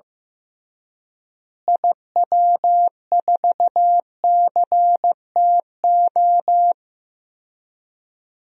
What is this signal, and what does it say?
Morse code, heard as a keyed tone, about 15 wpm, 700 Hz. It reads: IW4CTO
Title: DXCC-100 Call Signs - 6 Characters - Repeated 3x - 15wpm